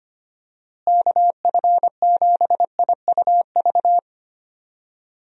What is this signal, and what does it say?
Morse code, heard as a keyed tone, about 25 wpm, 700 Hz. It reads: KF7IUV
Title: USA Call Signs - 6 Characters - Repeated 2x - 25wpm